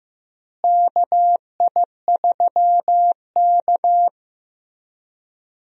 Morse code, heard as a keyed tone, about 15 wpm, 700 Hz. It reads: KI3K
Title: USA Call Signs - 4 Characters - Repeated 3x - 15wpm